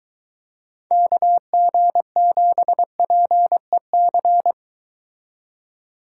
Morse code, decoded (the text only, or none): KG7PEC